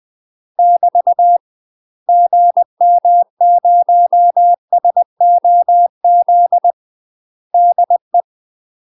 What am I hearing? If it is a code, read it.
Morse code, heard as a keyed tone, about 20 wpm, 700 Hz. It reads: <BT> GM0SOZ DE
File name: UK MASTER QSOs TRAINING / 20WPM / UK MASTER QSOs 0066